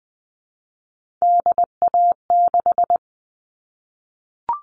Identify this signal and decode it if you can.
Morse code, heard as a keyed tone, about 20 wpm, 700 Hz. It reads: DA6
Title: All Cut Numbers - 3-Digit 20wpm